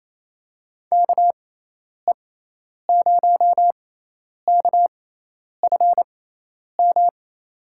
Morse code, heard as a keyed tone, about 28 wpm, spaced as for 10 wpm, 700 Hz. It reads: KE0KFM
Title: USA Call Signs - 6 Characters - 6x Character Spacing - 28wpm